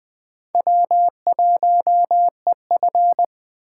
Morse code, heard as a keyed tone, about 21 wpm, 700 Hz. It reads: W1EF